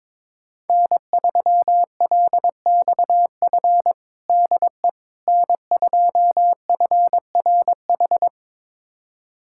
Morse code, heard as a keyed tone, about 22 wpm, 700 Hz. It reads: N3LXF DE N2FRH
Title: USA Callsign DE Callsign - 5 Characters - 22wpm